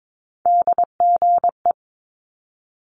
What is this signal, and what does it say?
Morse code, heard as a keyed tone, about 22 wpm, 700 Hz. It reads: DGE
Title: Multi-letter Phonemes Quick Fire 22wpm